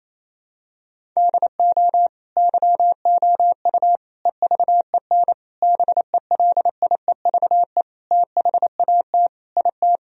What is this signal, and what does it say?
Morse code, heard as a keyed tone, about 28 wpm, 700 Hz. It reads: DO YOU EVEN BELIEVE THAT IT
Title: Sentences from Top 1000 Words - Review Part 1 - 28wpm